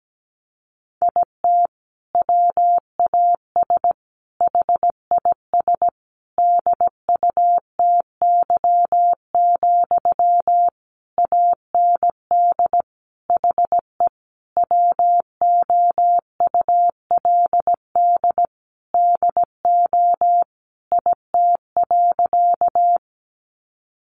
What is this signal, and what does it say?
Morse code, heard as a keyed tone, about 17 wpm, 700 Hz. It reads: IT WAS HIS DUTY, AND HE WOULD DO IT.
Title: Sentences from Top 1500 Words 17wpm